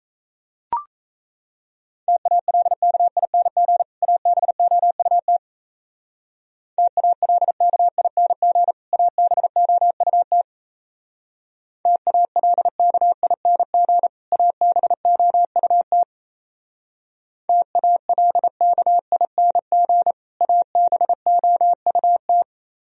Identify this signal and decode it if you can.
Morse code, heard as a keyed tone, about 34 wpm, 700 Hz. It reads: TALKING ABOUT TALKING ABOUT TALKING ABOUT TALKING ABOUT
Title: Speed-Racing - Sets of 2 Words 28wpm